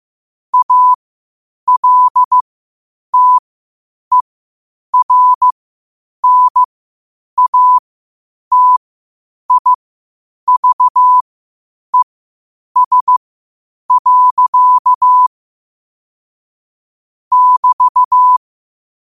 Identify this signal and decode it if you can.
Morse code, heard as a keyed tone, about 15 wpm, spaced as for 8 wpm, 1000 Hz. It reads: ALTERNATIVES. <BT>
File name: qotdfiles / 8wpm